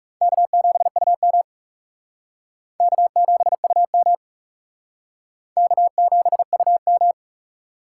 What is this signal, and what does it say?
Morse code, heard as a keyed tone, about 40 wpm, 700 Hz. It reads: K7UM K7UM K7UM